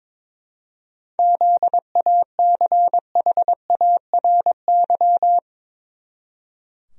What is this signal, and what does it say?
Morse code, heard as a keyed tone, about 22 wpm, 700 Hz. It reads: ZACHARY
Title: USA Common Names - 22wpm